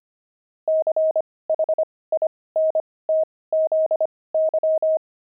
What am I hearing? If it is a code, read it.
Morse code, heard as a keyed tone, about 25 wpm, spaced as for 18 wpm, 630 Hz. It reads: CHINTZY
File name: pangram-fs-18-cs-25-peeves